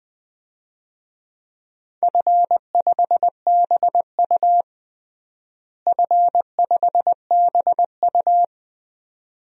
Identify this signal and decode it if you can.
Morse code, heard as a keyed tone, about 20 wpm, 700 Hz. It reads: F5BU F5BU
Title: DXCC-100 Call Signs - 4 Characters - Repeated 2x - 20wpm